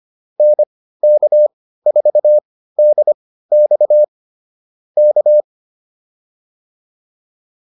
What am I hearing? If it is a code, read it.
Morse code, heard as a keyed tone, about 25 wpm, spaced as for 15 wpm, 600 Hz. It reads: NK4DX K
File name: QSO110 15